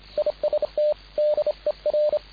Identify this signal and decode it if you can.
Morse code, heard as a keyed tone, about 29 wpm, 600 Hz, with radio noise. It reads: IST DER
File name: dk5ke / BUG-Beispiel